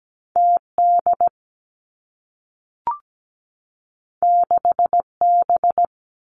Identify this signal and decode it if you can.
Morse code, heard as a keyed tone, about 17 wpm, 700 Hz. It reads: T D 6 B
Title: Two Challenging Letter-Number Combinations 17wpm